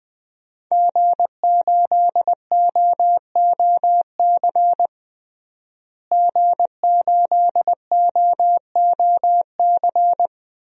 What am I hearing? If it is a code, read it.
Morse code, heard as a keyed tone, about 20 wpm, 700 Hz. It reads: G8OOC G8OOC
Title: DXCC-100 Call Signs - 5 Characters - Repeated 3x - 20wpm